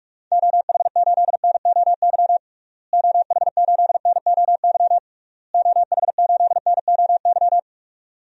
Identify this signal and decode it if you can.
Morse code, heard as a keyed tone, about 46 wpm, 700 Hz. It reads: OH8NOY OH8NOY OH8NOY
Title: DXCC-100 Call Signs - 6 Characters - Repeated 3x - 45wpm